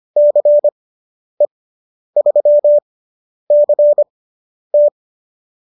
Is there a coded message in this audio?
Yes: Morse code from 0.2 to 4.9 s, heard as a keyed tone, about 25 wpm, spaced as for 10 wpm, 590 Hz. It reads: CE3CT